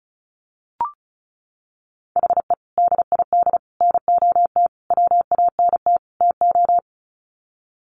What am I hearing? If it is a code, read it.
Morse code, heard as a keyed tone, about 34 wpm, 700 Hz. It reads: HE DID NOT WANT TO
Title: Sets of 5 Words 35wpm